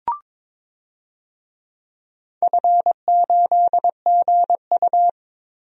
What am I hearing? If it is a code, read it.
Morse code, heard as a keyed tone, about 22 wpm, 700 Hz. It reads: F8GU
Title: DXCC-100 Call Signs - 4 Characters - 22wpm